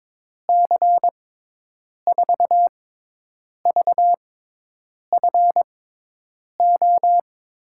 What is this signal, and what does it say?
Morse code, heard as a keyed tone, about 22 wpm, spaced as for 8 wpm, 700 Hz. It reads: C4VFO
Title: USA Call Signs - 6 Characters - 6x Character Spacing - 22wpm